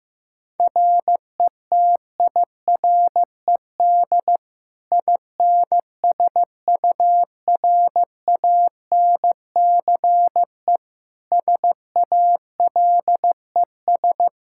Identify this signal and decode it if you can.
Morse code, heard as a keyed tone, about 16 wpm, 700 Hz. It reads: RETIRED INSURANCE SALES
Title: Practice QSO - Exchange 3 - More Info - 15wpm